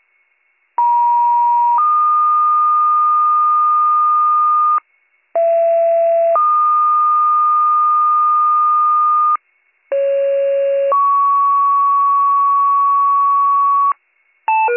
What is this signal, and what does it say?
Two-tone page, heard as a one-second tone, then a three-second tone, about 950 then 1220 Hz, about 670 then 1220 Hz, about 550 then 1120 Hz. It is heard from 0.8 to 13.9 s.